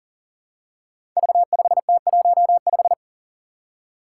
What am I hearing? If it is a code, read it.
Morse code, heard as a keyed tone, about 43 wpm, 700 Hz. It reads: V5T15
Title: Five Challenging Letter-Number Combinations 40wpm